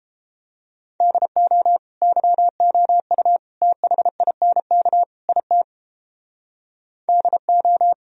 Morse code, heard as a keyed tone, about 33 wpm, 700 Hz. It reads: DO YOU THINK IT DO